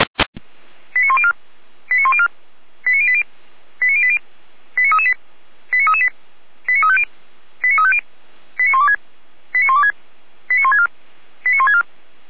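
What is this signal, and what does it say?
Five-tone selective call (ZVEI), heard as quick runs of five tones, tones of about 70 ms. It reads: 89104 89104 89080 89080 89338 89338 89350 89350 89126 89126 89164 89164
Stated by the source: ZVEI five-tone selective call received via SDR.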